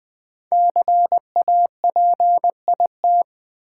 Morse code, heard as a keyed tone, about 20 wpm, 700 Hz. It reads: CAPIT
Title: Build A Word - 5 Letters At A Time - 10-Letter Words - 20wpm